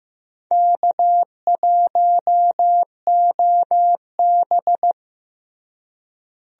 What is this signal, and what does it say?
Morse code, heard as a keyed tone, about 15 wpm, 700 Hz. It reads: K1OB
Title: USA Call Signs - 4 Characters - Repeated 2x - 15wpm